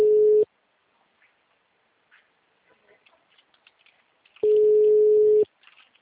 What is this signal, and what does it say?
Ringback tone (Europe, 425 Hz).
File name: out- -2037- - - .23822